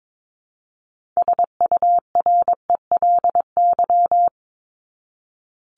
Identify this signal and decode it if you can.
Morse code, heard as a keyed tone, about 22 wpm, 700 Hz. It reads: SURELY
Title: LY Suffix - Top 100 Words with Base Words 22wpm